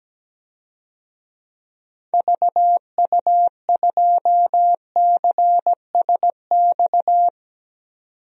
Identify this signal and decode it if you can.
Morse code, heard as a keyed tone, about 17 wpm, 700 Hz. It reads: VU2CSX
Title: DXCC-100 Call Signs - 6 Characters - 17wpm